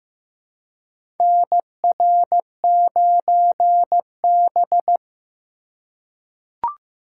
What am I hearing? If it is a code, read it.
Morse code, heard as a keyed tone, about 15 wpm, 700 Hz. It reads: NR9B
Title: USA Call Signs - 4 Characters - 4x Character Spacing - 15wpm